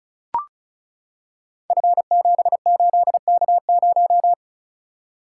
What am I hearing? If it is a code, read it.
Morse code, heard as a keyed tone, about 36 wpm, 690 Hz. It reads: F78K0
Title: Code Groups - 5 Letter:Numbers - 35wpm